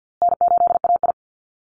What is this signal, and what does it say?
Morse code, heard as a keyed tone, about 50 wpm, 700 Hz. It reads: N8AI
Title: Call Signs Repeated 3x - 50wpm